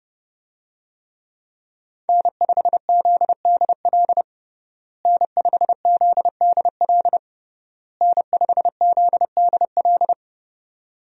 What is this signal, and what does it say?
Morse code, heard as a keyed tone, about 30 wpm, 700 Hz. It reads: N5ZDL N5ZDL N5ZDL
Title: USA Call Signs - 5 Characters - Repeated 3x - 30wpm